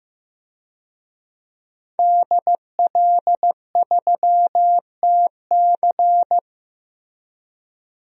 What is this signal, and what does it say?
Morse code, heard as a keyed tone, about 15 wpm, 700 Hz. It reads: DL3TC
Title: DXCC-100 Call Signs - 5 Characters - Repeated 3x - 15wpm